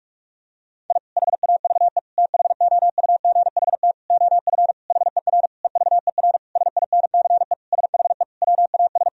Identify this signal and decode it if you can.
Morse code, heard as a keyed tone, about 46 wpm, 700 Hz. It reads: I HAVE THOUGHT OF HER EVER SINCE SHE WAS
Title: Sentences from Top 1000 Words - Review Part 1 - 45wpm